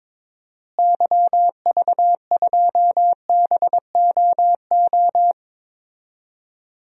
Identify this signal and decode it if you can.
Morse code, heard as a keyed tone, about 22 wpm, 700 Hz. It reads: YV2BOO
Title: DXCC-100 Call Signs - 6 Characters - Repeated 2x - 22wpm